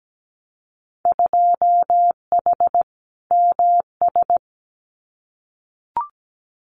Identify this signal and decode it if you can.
Morse code, heard as a keyed tone, about 17 wpm, 700 Hz. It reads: 2H MS
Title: Winter Field Day Exchanges 17wpm